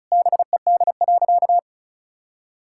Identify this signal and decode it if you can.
Morse code, heard as a keyed tone, about 34 wpm, 700 Hz. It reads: BED.
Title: Sentences from Top 1000 Words - Review Part 6 - 35wpm